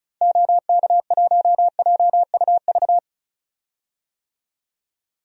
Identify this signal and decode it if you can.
Morse code, heard as a keyed tone, about 34 wpm, 700 Hz. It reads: OK1JUV
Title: DXCC-100 Call Signs - 6 Characters - 35wpm